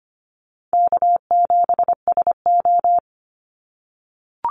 Morse code, heard as a keyed tone, about 25 wpm, 700 Hz. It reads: K7SO
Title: NA SOTA Activator Callsigns 25wpm